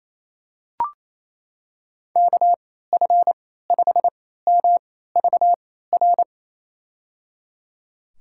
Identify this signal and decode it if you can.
Morse code, heard as a keyed tone, about 28 wpm, spaced as for 16 wpm, 700 Hz. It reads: KF5MVR
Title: USA Call Signs - 6 Characters - 3x Character Spacing - 28wpm